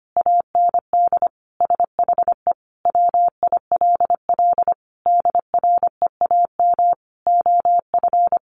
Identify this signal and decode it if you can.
Morse code, heard as a keyed tone, about 25 wpm, 700 Hz. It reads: AND SHE WILL DREAM OF
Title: Sentences from Top 1500 Words 25wpm